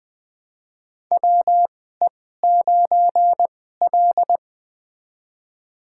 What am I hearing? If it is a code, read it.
Morse code, heard as a keyed tone, about 20 wpm, spaced as for 14 wpm, 700 Hz. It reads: WE9L
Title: USA Call Signs - 4 Characters - 2x Character Spacing - 20wpm